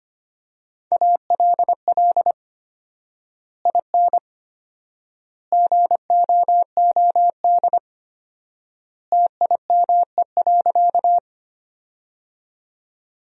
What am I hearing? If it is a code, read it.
Morse code, heard as a keyed tone, about 25 wpm, 700 Hz. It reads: ALL IN GOOD TIME.
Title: 4x Word-Spacing - Sentences from Top 200 Words - 25wpm